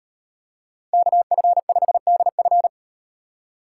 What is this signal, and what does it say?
Morse code, heard as a keyed tone, about 38 wpm, 700 Hz. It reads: KF5DF